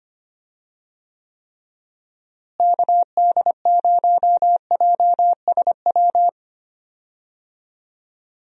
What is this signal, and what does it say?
Morse code, heard as a keyed tone, about 25 wpm, 700 Hz. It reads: KD0JSW
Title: USA Call Signs - 6 Characters - Repeated 3x - 25wpm